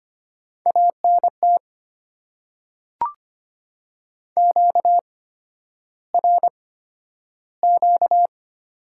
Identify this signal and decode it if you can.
Morse code, heard as a keyed tone, about 25 wpm, spaced as for 11 wpm, 690 Hz. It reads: ANT QRQ
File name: Top 90 QSO Elements - 8x Character Spacing - 25wpm